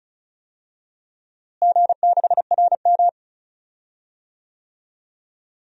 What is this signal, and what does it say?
Morse code, heard as a keyed tone, about 36 wpm, 700 Hz. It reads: G6RM
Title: DXCC-100 Call Signs - 4 Characters - Repeated 3x - 35wpm